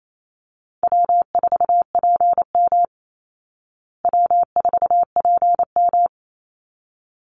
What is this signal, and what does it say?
Morse code, heard as a keyed tone, about 28 wpm, 700 Hz. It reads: W4PM W4PM
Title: Easy Call Signs Repeated 2x - 28wpm